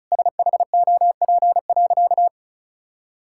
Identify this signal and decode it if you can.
Morse code, heard as a keyed tone, about 34 wpm, 700 Hz. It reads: SHOP.
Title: Sentences from Top 1600 Words 35wpm